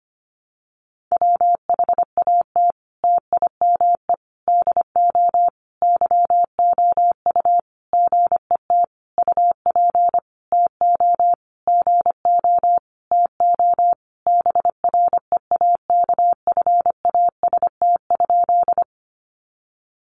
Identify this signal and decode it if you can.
Morse code, heard as a keyed tone, about 25 wpm, 700 Hz. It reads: WHAT TIME DO YOU GET UP TO GO TO BREAKFAST?
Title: Sentences from Top 1500 Words 25wpm